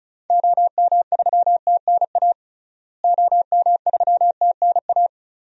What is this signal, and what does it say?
Morse code, heard as a keyed tone, about 36 wpm, 700 Hz. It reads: OM3TNA OM3TNA
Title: DXCC-100 Call Signs - 6 Characters - Repeated 3x - 35wpm